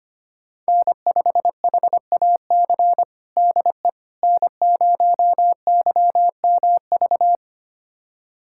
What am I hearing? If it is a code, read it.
Morse code, heard as a keyed tone, about 25 wpm, 700 Hz. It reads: N5HAC DE N0YMV
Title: USA Callsign DE Callsign - 5 Characters - 25wpm